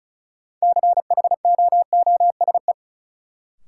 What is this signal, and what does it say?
Morse code, heard as a keyed tone, about 36 wpm, 700 Hz. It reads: CHOOSE
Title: Sentences from Top 700 Words - New Word Drill - 35wpm